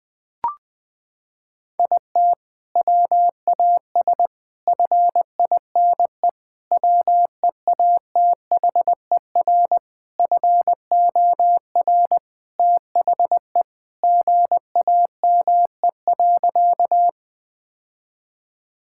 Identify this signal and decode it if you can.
Morse code, heard as a keyed tone, about 21 wpm, 700 Hz. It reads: IT WAS FINE WEATHER FOR THE GAME.
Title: Sentences from Top 1000 Words - Review Part 5 - 20wpm